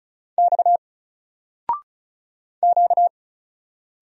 Morse code, heard as a keyed tone, about 36 wpm, 700 Hz. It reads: X Q
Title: Single Letter-Number 35wpm